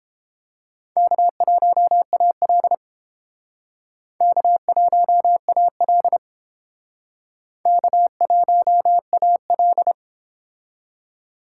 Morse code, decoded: K1AL K1AL K1AL

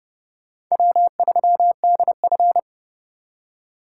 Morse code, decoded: W3DF

